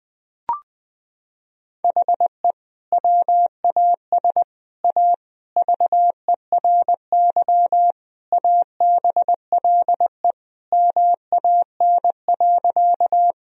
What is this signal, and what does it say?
Morse code, heard as a keyed tone, about 20 wpm, 700 Hz. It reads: HE WAS A VERY ABLE MAN.
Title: Sentences from Top 1000 Words - Review Part 2 - 20wpm